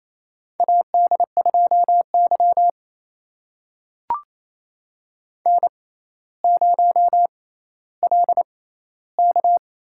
Morse code, heard as a keyed tone, about 28 wpm, spaced as for 14 wpm, 700 Hz. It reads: AD2Y N0LK
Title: USA Call Signs - 4 Characters - 6x Character Spacing - 28wpm